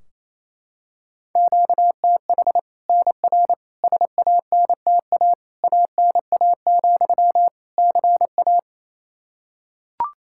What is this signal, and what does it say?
Morse code, heard as a keyed tone, about 27 wpm, 700 Hz. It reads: QTH NR SANTA ANA, CA
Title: Practice QSO - Exchange 1 - Beginning - 28wpm